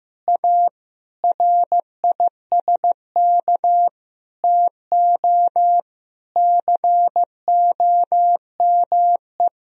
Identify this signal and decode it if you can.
Morse code, heard as a keyed tone, about 15 wpm, 700 Hz. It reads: A RISK TO COME
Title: Sentences from Top 1000 Words - Review Part 4 - 15wpm